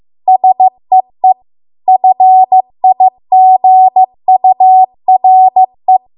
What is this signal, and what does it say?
Morse code, heard as a keyed tone, about 15 wpm, 750 Hz. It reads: SEE FIGURE